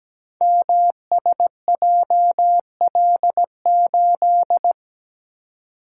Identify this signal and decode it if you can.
Morse code, heard as a keyed tone, about 17 wpm, 700 Hz. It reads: MSJL8
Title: Code Groups - 5 Letter:Numbers - 17wpm